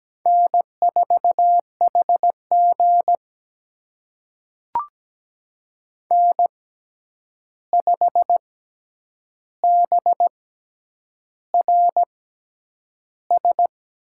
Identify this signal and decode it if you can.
Morse code, heard as a keyed tone, about 17 wpm, spaced as for 8 wpm, 700 Hz. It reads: N4HG N5BRS
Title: USA Call Signs - 6x Character Spacing - 17wpm